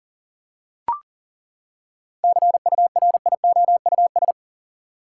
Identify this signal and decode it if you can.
Morse code, heard as a keyed tone, about 40 wpm, 700 Hz. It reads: CURIOUS
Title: Seven Letter Words - 40wpm